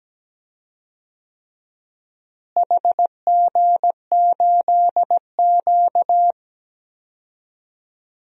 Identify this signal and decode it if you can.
Morse code, heard as a keyed tone, about 17 wpm, 700 Hz. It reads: HG8Q
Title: DXCC-100 Call Signs - 4 Characters - Repeated 3x - 17wpm